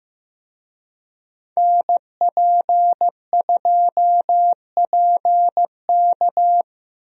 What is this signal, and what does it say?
Morse code, heard as a keyed tone, about 15 wpm, 700 Hz. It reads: NP2PK
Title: USA Call Signs - 5 Characters - Repeated 2x - 15wpm